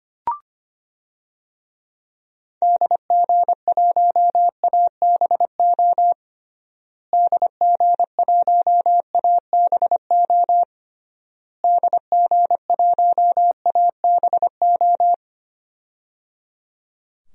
Morse code, heard as a keyed tone, about 25 wpm, 700 Hz. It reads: DG1ABO DG1ABO DG1ABO
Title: DXCC-100 Call Signs - 6 Characters - Repeated 3x - 25wpm